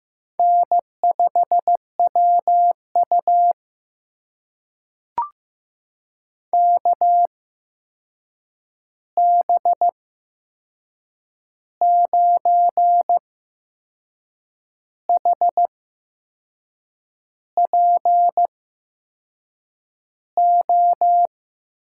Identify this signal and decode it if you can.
Morse code, heard as a keyed tone, about 15 wpm, spaced as for 6 wpm, 700 Hz. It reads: N5WU KB9HPO
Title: USA Call Signs - 8x Character Spacing - 15wpm